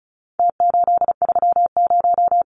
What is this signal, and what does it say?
Morse code, heard as a keyed tone, about 34 wpm, 700 Hz. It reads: T830
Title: Numbers - Contest Style - 4-Digit 35wpm